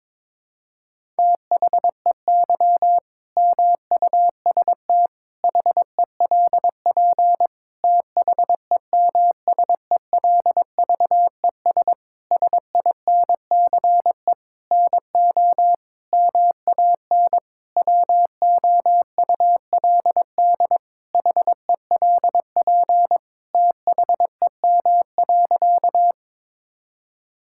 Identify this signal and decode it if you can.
Morse code, heard as a keyed tone, about 22 wpm, 700 Hz. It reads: THEY MUST HELP THEMSELVES SINCE NO MAN WOULD HELP THEM.
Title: Sentences from Top 1000 Words - Review Part 5 - 22wpm